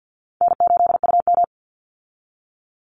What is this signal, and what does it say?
Morse code, heard as a keyed tone, about 50 wpm, 700 Hz. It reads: N8UM
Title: Easy Call Signs Repeated 2x - 50wpm